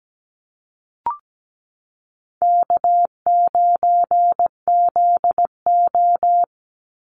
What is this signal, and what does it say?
Morse code, heard as a keyed tone, about 17 wpm, 700 Hz. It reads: K9ZO
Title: Easy Call Signs Repeated 3x - 17wpm